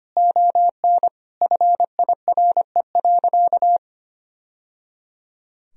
Morse code, heard as a keyed tone, about 25 wpm, 700 Hz. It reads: ON FIRE.
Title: Sentences from Top 1000 Words - Review Part 2 - 25wpm